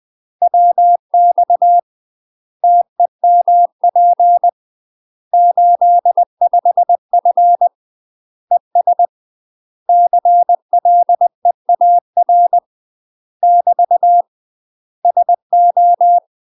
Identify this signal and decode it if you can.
Morse code, heard as a keyed tone, about 20 wpm, 700 Hz. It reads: WX TEMP 85F ES CLEAR <BT> SO
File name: UK MASTER QSOs TRAINING / 20WPM / UK MASTER QSOs 0028